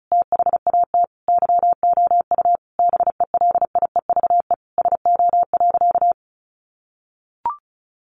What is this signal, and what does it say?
Morse code, heard as a keyed tone, about 34 wpm, 700 Hz. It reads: THAT YOU BELIEVE SO.
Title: Sentences from Top 1000 Words 35wpm